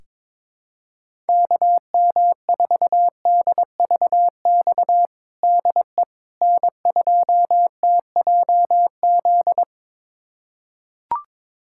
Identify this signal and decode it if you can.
Morse code, heard as a keyed tone, about 22 wpm, 700 Hz. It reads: KM4DVX DE N2TJZ
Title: USA Callsign DE Callsign 22wpm